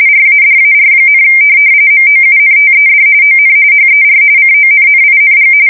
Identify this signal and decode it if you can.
RTTY, about 45.45 baud. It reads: RADAR, MOUNTED ON HER CONNING